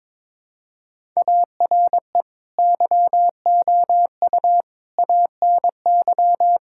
Morse code, heard as a keyed tone, about 22 wpm, 700 Hz. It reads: ARE YOU ANY
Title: Sentences from Top 1000 Words - Review Part 6 - 22wpm